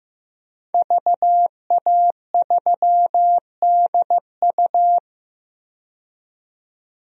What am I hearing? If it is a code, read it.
Morse code, heard as a keyed tone, about 15 wpm, 700 Hz. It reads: VA3DU
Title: DXCC-100 Call Signs - 5 Characters - 15wpm